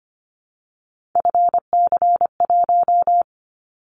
Morse code, heard as a keyed tone, about 25 wpm, 700 Hz. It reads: FC1
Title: Callsign Prefixes for Top 25 DXCC - Repeated 3x - 25wpm